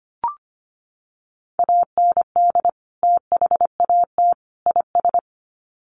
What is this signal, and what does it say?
Morse code, heard as a keyed tone, about 25 wpm, 700 Hz. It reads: AND THAT IS
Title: Sets of 3 Words 25wpm